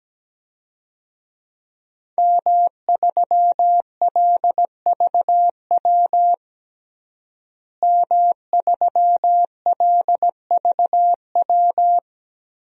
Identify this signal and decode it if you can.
Morse code, heard as a keyed tone, about 17 wpm, 700 Hz. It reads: M3LVW M3LVW
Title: DXCC-100 Call Signs - 5 Characters - Repeated 3x - 17wpm